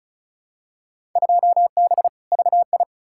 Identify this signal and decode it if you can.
Morse code, heard as a keyed tone, about 36 wpm, 700 Hz. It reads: 2B VI
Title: Field Day Exchanges Encore - 35wpm